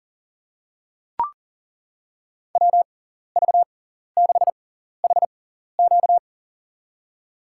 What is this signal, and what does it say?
Morse code, heard as a keyed tone, about 38 wpm, spaced as for 14 wpm, 700 Hz. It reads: WV6HQ
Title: USA Call Signs - 5 Characters - 6x Character Spacing - 40wpm